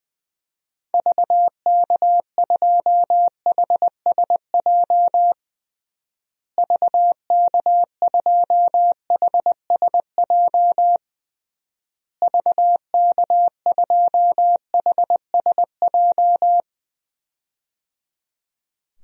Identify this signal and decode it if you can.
Morse code, heard as a keyed tone, about 20 wpm, 700 Hz. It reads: VK2HSJ VK2HSJ VK2HSJ
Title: DXCC-100 Call Signs - 6 Characters - Repeated 3x - 20wpm